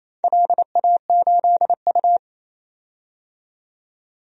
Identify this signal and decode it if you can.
Morse code, heard as a keyed tone, about 29 wpm, 700 Hz. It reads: LA8U
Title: DXCC-100 Call Signs - 4 Characters - 28wpm